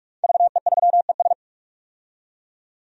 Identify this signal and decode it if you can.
Morse code, heard as a keyed tone, about 42 wpm, 700 Hz. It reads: VE3ES
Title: DXCC-100 Call Signs - 5 Characters - 45wpm